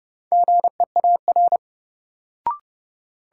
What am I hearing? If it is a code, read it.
Morse code, heard as a keyed tone, about 31 wpm, 700 Hz. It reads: GEAR
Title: Sentences from Top 1400 Words - New Word Drill Alt - 30wpm